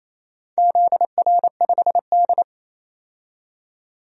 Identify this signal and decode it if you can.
Morse code, heard as a keyed tone, about 28 wpm, 700 Hz. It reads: ZR5D
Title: DXCC-100 Call Signs - 4 Characters - 28wpm